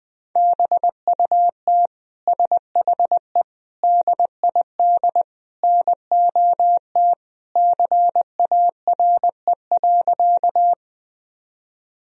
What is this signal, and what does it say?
Morse code, heard as a keyed tone, about 21 wpm, 700 Hz. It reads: BUT SHE DID NOT CARE.
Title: Sentences from Top 1000 Words - Review Part 1 - 20wpm